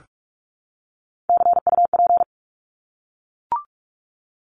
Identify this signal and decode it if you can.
Morse code, heard as a keyed tone, about 44 wpm, 700 Hz. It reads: CUP
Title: Top 1501-1750 Words 45wpm